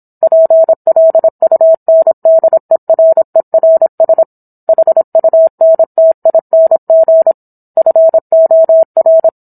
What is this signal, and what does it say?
Morse code, heard as a keyed tone, about 27 wpm, 650 Hz. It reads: PLUNDERERS HUNTING FOR